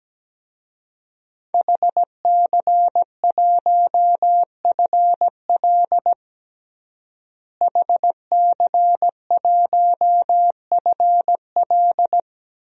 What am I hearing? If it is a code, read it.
Morse code, heard as a keyed tone, about 17 wpm, 700 Hz. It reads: HC1FL HC1FL
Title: DXCC-100 Call Signs - 5 Characters - Repeated 2x - 17wpm